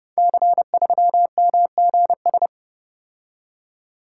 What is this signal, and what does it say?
Morse code, heard as a keyed tone, about 30 wpm, 700 Hz. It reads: C3MGS
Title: Code Groups - 5 Letter:Numbers - 30wpm